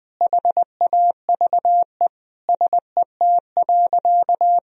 Morse code, heard as a keyed tone, about 20 wpm, 700 Hz. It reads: HAVE SET.
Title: Sentences from Top 1000 Words - Review Part 2 - 20wpm